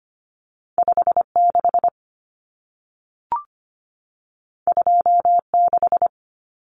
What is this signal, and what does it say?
Morse code, heard as a keyed tone, about 25 wpm, 700 Hz. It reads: 56 26
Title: Numbers - 2-Digit 25wpm